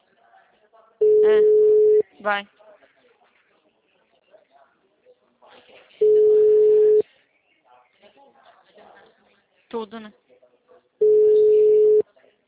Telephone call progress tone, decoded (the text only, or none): ringback tone